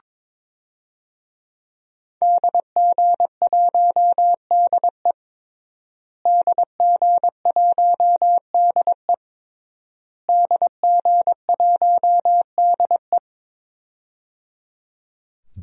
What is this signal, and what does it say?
Morse code, heard as a keyed tone, about 22 wpm, 700 Hz. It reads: DG1DE DG1DE DG1DE
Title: DXCC-100 Call Signs - 5 Characters - Repeated 3x - 22wpm